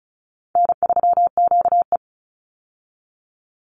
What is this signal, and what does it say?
Morse code, heard as a keyed tone, about 34 wpm, 700 Hz. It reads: N3QE
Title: Easy Call Signs Repeated 2x - 35wpm